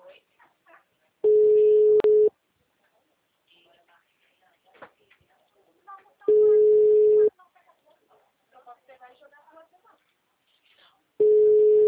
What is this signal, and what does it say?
Ringback tone (Europe, 425 Hz), with speech.